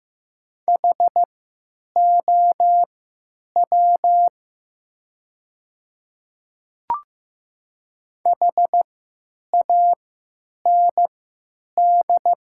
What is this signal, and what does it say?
Morse code, heard as a keyed tone, about 15 wpm, spaced as for 9 wpm, 700 Hz. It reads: HOW HAND
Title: 3x Character Spacing - Top 200 Words - 15wpm